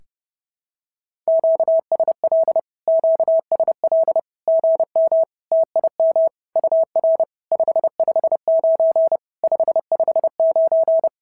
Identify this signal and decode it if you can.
Morse code, heard as a keyed tone, about 30 wpm, 650 Hz. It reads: QSL QSL GM TIM UR 559 559
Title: SOTA QSO Practice 30wpm